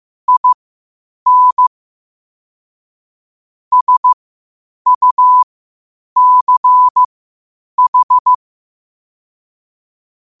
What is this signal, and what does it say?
Morse code, heard as a keyed tone, about 15 wpm, spaced as for 8 wpm, 1000 Hz. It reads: IN SUCH
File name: qotdfiles / 8wpm